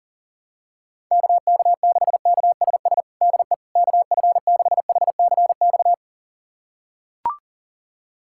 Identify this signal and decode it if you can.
Morse code, heard as a keyed tone, about 40 wpm, 700 Hz. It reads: KK6KSS DE KF6HCX
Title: USA Callsign DE Callsign - 6 Characters - 40wpm